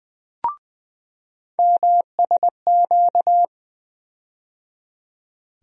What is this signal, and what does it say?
Morse code, heard as a keyed tone, about 20 wpm, 700 Hz. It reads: MSQ
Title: USA Call Sign Postfixes - 3 Letters 20wpm